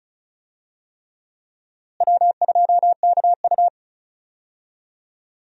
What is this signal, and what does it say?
Morse code, heard as a keyed tone, about 34 wpm, 700 Hz. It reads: W2KU
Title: USA Call Signs - 4 Characters - Repeated 3x - 35wpm